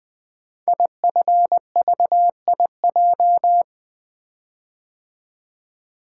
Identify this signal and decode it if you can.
Morse code, heard as a keyed tone, about 20 wpm, 700 Hz. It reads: IFVIJ
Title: Code Groups - 5 Letters - 20wpm